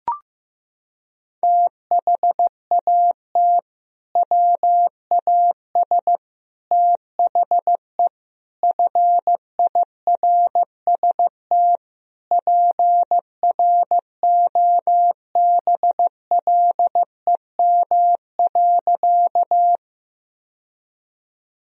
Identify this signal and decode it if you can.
Morse code, heard as a keyed tone, about 15 wpm, 700 Hz. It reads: THAT WAS THE FIRST PROBLEM.
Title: Sentences from Top 1000 Words - Review Part 2 - 15wpm